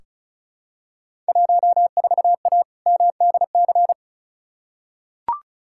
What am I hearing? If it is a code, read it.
Morse code, heard as a keyed tone, about 36 wpm, 700 Hz. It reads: 14A MDC
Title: Field Day Exchanges Encore - 35wpm